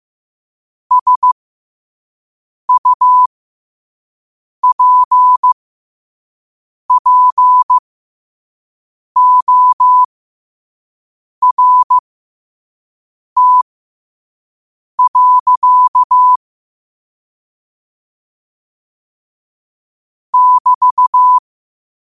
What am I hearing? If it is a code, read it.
Morse code, heard as a keyed tone, about 15 wpm, spaced as for 5 wpm, 1000 Hz. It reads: SUPPORT. <BT>
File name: qotdfiles / 5wpm